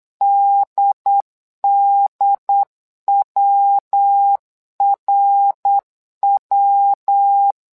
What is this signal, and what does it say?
Morse code, heard as a keyed tone, about 8 wpm, 800 Hz. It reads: DDWRW
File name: LEZIONI / 3 MWRO / 3 MWRO 8wpm 2606